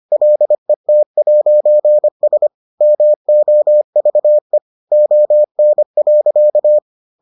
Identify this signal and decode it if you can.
Morse code, heard as a keyed tone, about 25 wpm, 600 Hz. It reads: LET'S MOVE ON.